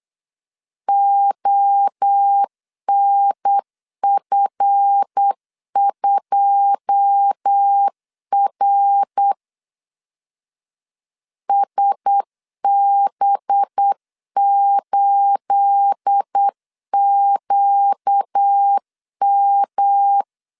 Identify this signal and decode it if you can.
Morse code, heard as a keyed tone, about 8 wpm, 800 Hz. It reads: ONF2R SB8QM
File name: CW / DET / 1 9 1500C S B 8wpm 5624